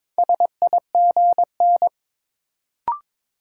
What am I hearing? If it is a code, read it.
Morse code, heard as a keyed tone, about 22 wpm, 700 Hz. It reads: SIGN
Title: Sentences from Top 1000 Words - New Word Drill - 22wpm